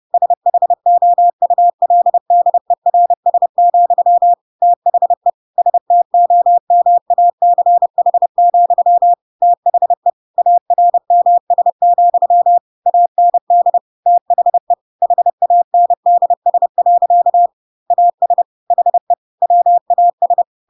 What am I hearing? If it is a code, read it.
Morse code, heard as a keyed tone, about 30 wpm, 700 Hz. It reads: SHOULDERS, THE STOMACH, THE ARMS, AND THE HANDS. AS HE WAS
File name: Pinocchio / Pinocchio ch3 30 30 ik1hge